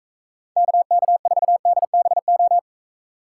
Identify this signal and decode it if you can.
Morse code, heard as a keyed tone, about 42 wpm, 700 Hz. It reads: KK4DBO